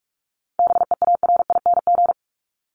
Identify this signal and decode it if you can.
Morse code, heard as a keyed tone, about 44 wpm, 700 Hz. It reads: BEARING
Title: ING Suffix - Top 100 Words with Base Words 45wpm